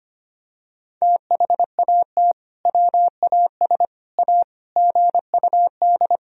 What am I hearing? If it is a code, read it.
Morse code, heard as a keyed tone, about 25 wpm, 700 Hz. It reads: THAT WAS A GUD
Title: Sentences from Top 200 Words and Abbreviations 25wpm